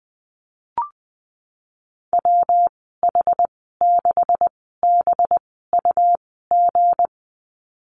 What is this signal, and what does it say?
Morse code, heard as a keyed tone, about 20 wpm, spaced as for 15 wpm, 700 Hz. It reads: WH6BUG
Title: 193 Lesson 31g - Letter v - Call Signs 20wpm